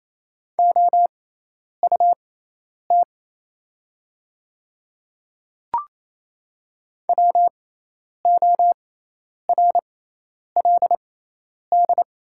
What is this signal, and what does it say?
Morse code, heard as a keyed tone, about 28 wpm, spaced as for 10 wpm, 700 Hz. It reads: OUT WORLD
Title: 6x Character Spacing - Top 200 Words - 28wpm